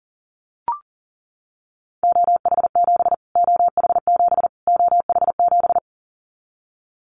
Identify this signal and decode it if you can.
Morse code, heard as a keyed tone, about 40 wpm, 700 Hz. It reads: OH7 OH7 OH7
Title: Callsign Prefixes for Top 25 DXCC - Repeated 3x - 40wpm